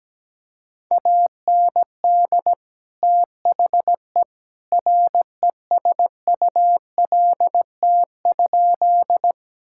Morse code, heard as a keyed tone, about 17 wpm, 700 Hz. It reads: AND THE RESULT?
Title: Sentences from Top 1000 Words - Review Part 5 - 17wpm